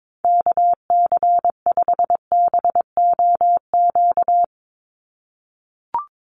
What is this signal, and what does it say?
Morse code, heard as a keyed tone, about 22 wpm, 700 Hz. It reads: KC5BOQ
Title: USA Call Signs 22wpm